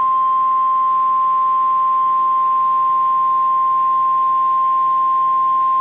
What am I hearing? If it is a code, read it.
Emergency Alert System (SAME) tones (the 1050 Hz attention tone). It is heard from 1.1 to 5.3 s.
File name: EAS Logs / KMNR